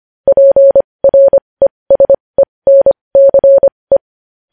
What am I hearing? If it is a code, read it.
Morse code, heard as a keyed tone, about 25 wpm, spaced as for 20 wpm, 550 Hz. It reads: PRESENCE
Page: News Headlines in Morse Code at 20 WPM